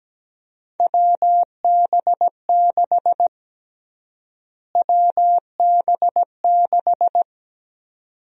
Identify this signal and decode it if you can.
Morse code, heard as a keyed tone, about 17 wpm, 700 Hz. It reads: WB6 WB6
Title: USA Call Sign Prefixes - Repeated 2x - 17wpm